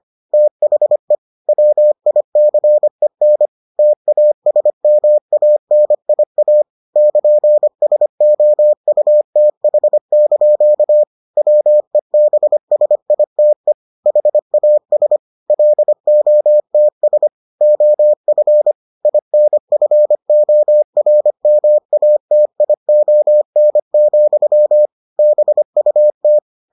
Morse code, heard as a keyed tone, about 25 wpm, 600 Hz. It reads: THE WICEN TASMANIA <KN>SOUTH) WEBSITE HAS LOTS OF INFORMATION, BUT